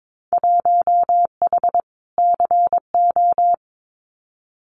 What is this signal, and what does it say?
Morse code, heard as a keyed tone, about 22 wpm, 700 Hz. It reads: 1H CO
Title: Winter Field Day Exchanges 22wpm